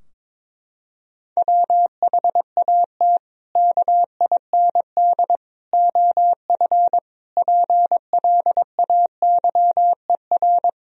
Morse code, heard as a keyed tone, about 22 wpm, 700 Hz. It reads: WHAT KIND OF PLAYER
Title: Sentences from Top 1000 Words - Review Part 4 - 22wpm